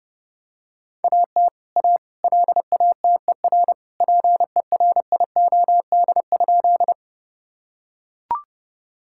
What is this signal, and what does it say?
Morse code, heard as a keyed tone, about 32 wpm, 700 Hz. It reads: AT A LATER PERIOD?
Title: Sentences from Top 1000 Words - Review Part 5 - 30wpm